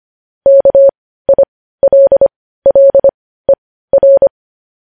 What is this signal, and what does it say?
Morse code, heard as a keyed tone, about 25 wpm, spaced as for 15 wpm, 550 Hz. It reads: KILLER
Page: news-headlines-in-morse-code-at-15-wpm